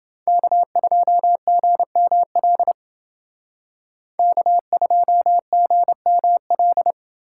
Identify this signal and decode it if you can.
Morse code, heard as a keyed tone, about 28 wpm, 700 Hz. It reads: K2GML K2GML